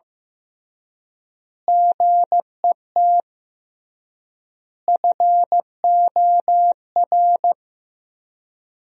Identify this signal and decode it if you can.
Morse code, heard as a keyed tone, about 15 wpm, 700 Hz. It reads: GET FOR
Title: 3x Word-Spacing - Sentences from Top 400 Words - 15wpm